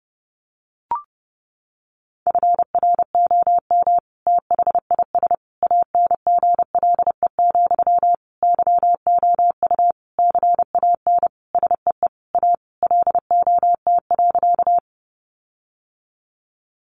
Morse code, heard as a keyed tone, about 28 wpm, 700 Hz. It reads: FROM THIS ANGLE, YOU CAN SEE A LOT.
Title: Sentences from Top 1500 Words 30wpm